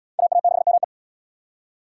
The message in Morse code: AA6ME

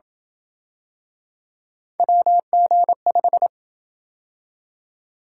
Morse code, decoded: WG5

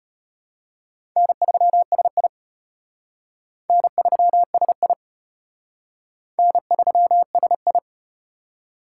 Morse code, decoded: N3SI N3SI N3SI

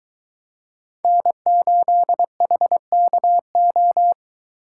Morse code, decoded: N8HKO

